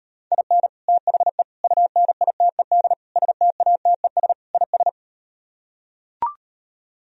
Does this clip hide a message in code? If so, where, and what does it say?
Yes: Morse code from 0.3 to 4.9 s, heard as a keyed tone, about 38 wpm, 700 Hz. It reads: IN THE UNITED STATES IS